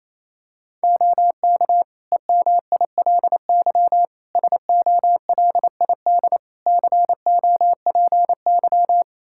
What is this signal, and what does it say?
Morse code, heard as a keyed tone, about 28 wpm, 700 Hz. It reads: OK EMILY SOLID COPY
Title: Practice QSO - Exchange 3 - More Info - 28wpm